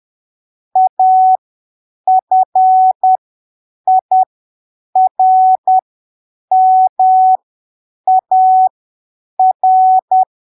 Morse code, heard as a keyed tone, about 10 wpm, spaced as for 7 wpm, 750 Hz. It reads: AFIRMAR
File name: arquivos / texto / texto 24 10ppm